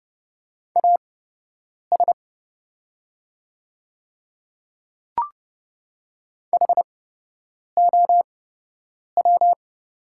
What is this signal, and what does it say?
Morse code, heard as a keyed tone, about 30 wpm, spaced as for 8 wpm, 700 Hz. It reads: AS HOW
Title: 8x Character Spacing - Top 200 Words - 30wpm